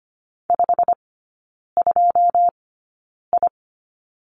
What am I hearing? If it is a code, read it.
Morse code, heard as a keyed tone, about 25 wpm, spaced as for 9 wpm, 700 Hz. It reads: 52I
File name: Lesson 17e - Letter u - Three Characters 25wpm